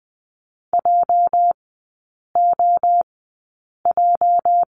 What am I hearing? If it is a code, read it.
Morse code, heard as a keyed tone, about 20 wpm, spaced as for 8 wpm, 700 Hz. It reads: JOJ
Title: 205 Lesson 33e - Letter j - Three Characters 20wpm